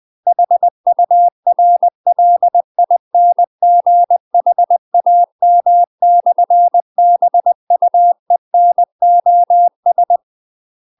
Morse code, heard as a keyed tone, about 20 wpm, 700 Hz. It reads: HURLINGHAM/BUENOS
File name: UK MASTER QSOs TRAINING / 20WPM / UK MASTER QSOs 0032